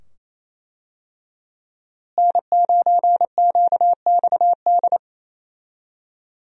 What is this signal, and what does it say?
Morse code, heard as a keyed tone, about 27 wpm, 700 Hz. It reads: N9QXD
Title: USA Call Signs - 5 Characters - Repeated 2x - 28wpm